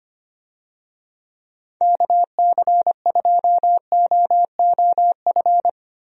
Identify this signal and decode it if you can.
Morse code, heard as a keyed tone, about 25 wpm, 700 Hz. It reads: KC2OOF
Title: USA Call Signs - 6 Characters - Repeated 3x - 25wpm